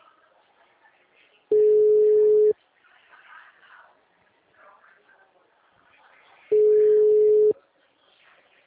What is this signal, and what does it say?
Ringback tone (Europe, 425 Hz).